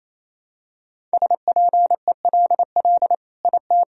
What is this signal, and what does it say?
Morse code, heard as a keyed tone, about 28 wpm, 700 Hz. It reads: SPELL IT
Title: Sentences from Top 1000 Words - Review Part 1 - 28wpm